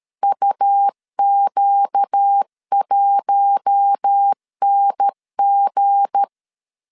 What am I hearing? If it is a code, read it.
Morse code, heard as a keyed tone, about 13 wpm, 800 Hz. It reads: UQ1NG
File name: CW / DET / 1 9 1500C S B 12wpm 3819